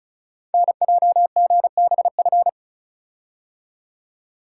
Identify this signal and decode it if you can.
Morse code, heard as a keyed tone, about 35 wpm, 690 Hz. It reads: NJGBF
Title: Code Groups - 5 Letters - 35wpm